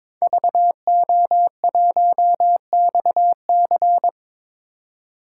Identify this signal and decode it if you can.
Morse code, heard as a keyed tone, about 22 wpm, 700 Hz. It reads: VO1XC
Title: K1USN SST QSO Practice - 22wpm